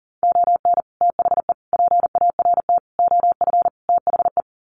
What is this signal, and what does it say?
Morse code, heard as a keyed tone, about 40 wpm, 700 Hz. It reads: ON THE PART OF THE
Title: Sets of 5 Words 40wpm